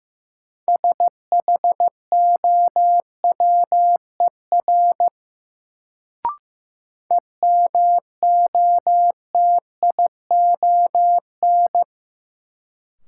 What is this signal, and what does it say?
Morse code, heard as a keyed tone, about 15 wpm, 700 Hz. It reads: SHOWER EMOTION
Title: Sentences from Top 1600 Words - New Word Drill - 15wpm